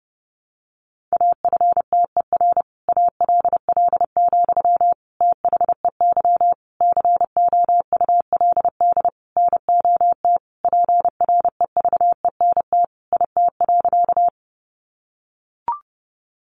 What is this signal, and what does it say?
Morse code, heard as a keyed tone, about 30 wpm, 700 Hz. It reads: AFTER ALL, THEY COULD NOT PREVENT IT.
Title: Sentences from Top 1100 Words 30wpm